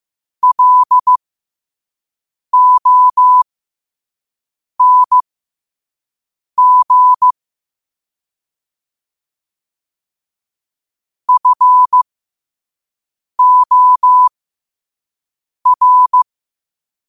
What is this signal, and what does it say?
Morse code, heard as a keyed tone, about 15 wpm, spaced as for 5 wpm, 1000 Hz. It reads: LONG FOR
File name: qotdfiles / 5wpm